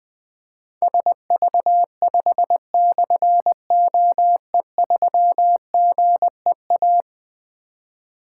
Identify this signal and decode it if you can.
Morse code, heard as a keyed tone, about 20 wpm, 700 Hz. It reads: SV5/OE3GEA
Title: Difficult Call Signs 20wpm